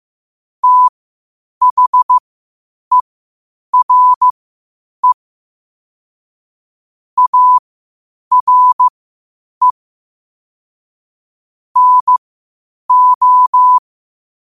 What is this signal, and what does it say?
Morse code, heard as a keyed tone, about 15 wpm, spaced as for 8 wpm, 1000 Hz. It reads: THERE ARE NO
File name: qotdfiles / 8wpm